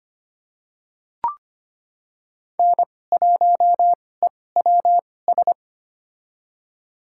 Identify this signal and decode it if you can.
Morse code, heard as a keyed tone, about 25 wpm, spaced as for 18 wpm, 700 Hz. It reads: N1EWS
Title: USA Call Signs - 5 Characters - 2x Character Spacing - 25wpm